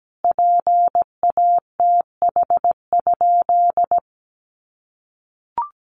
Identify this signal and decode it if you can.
Morse code, heard as a keyed tone, about 17 wpm, 700 Hz. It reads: PATH?
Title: Sentences from Top 1400 Words 17wpm